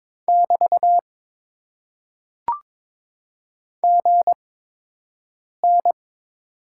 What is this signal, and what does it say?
Morse code, heard as a keyed tone, about 22 wpm, spaced as for 6 wpm, 700 Hz. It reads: <BT> GN
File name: Top 90 QSO Elements - 8x Character Spacing - 22wpm